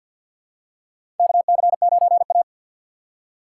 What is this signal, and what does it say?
Morse code, heard as a keyed tone, about 52 wpm, 700 Hz. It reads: KC9A
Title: USA Call Signs - 3x Character Spacing - 50wpm